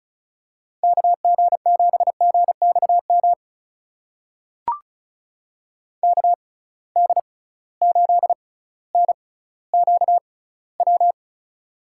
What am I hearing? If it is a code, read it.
Morse code, heard as a keyed tone, about 36 wpm, spaced as for 18 wpm, 700 Hz. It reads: KG7GXM KD8NQW
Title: USA Call Signs - 6 Characters - 6x Character Spacing - 35wpm